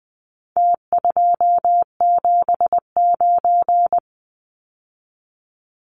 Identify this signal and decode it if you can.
Morse code, heard as a keyed tone, about 20 wpm, 700 Hz. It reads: T279
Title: Numbers - Contest Style - 4-Digit 20wpm